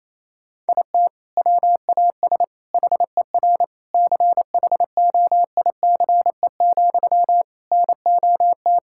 Morse code, heard as a keyed tone, about 28 wpm, 700 Hz. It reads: IT WAS HER CHOICE, NOT
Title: Sentences from Top 1000 Words - Review Part 6 - 28wpm